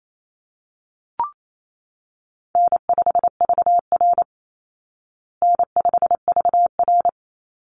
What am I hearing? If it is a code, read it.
Morse code, heard as a keyed tone, about 28 wpm, 700 Hz. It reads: N5VR N5VR
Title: Easy Call Signs Repeated 2x - 28wpm